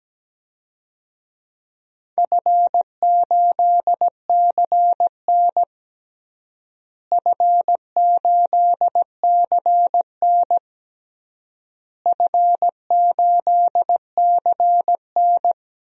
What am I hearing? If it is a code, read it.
Morse code, heard as a keyed tone, about 17 wpm, 700 Hz. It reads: F8CN F8CN F8CN
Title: DXCC-100 Call Signs - 4 Characters - Repeated 3x - 17wpm